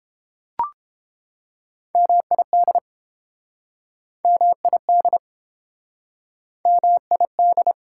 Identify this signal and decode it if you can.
Morse code, heard as a keyed tone, about 30 wpm, 700 Hz. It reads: MID MID MID